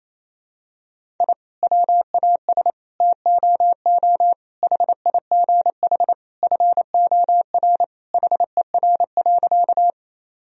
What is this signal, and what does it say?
Morse code, heard as a keyed tone, about 28 wpm, 700 Hz. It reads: I WAS TOO HIGH FOR HER.
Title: Sentences from Top 1000 Words - Review Part 1 - 28wpm